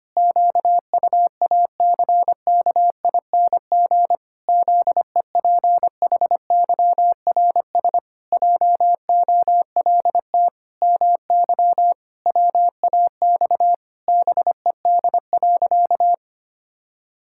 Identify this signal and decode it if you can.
Morse code, heard as a keyed tone, about 25 wpm, 700 Hz. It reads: QUACKING ZEPHYRS JOLT MY WAX BED.
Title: Pangram Challenge v2 - 25wpm